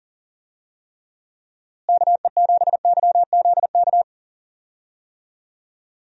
Morse code, heard as a keyed tone, about 40 wpm, 700 Hz. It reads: KE7YZK
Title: USA Call Signs - 6 Characters - Repeated 2x - 40wpm